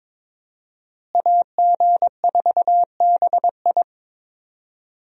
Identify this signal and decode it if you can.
Morse code, heard as a keyed tone, about 22 wpm, 700 Hz. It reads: AG4BI
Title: USA Call Signs - 5 Characters - Repeated 3x - 22wpm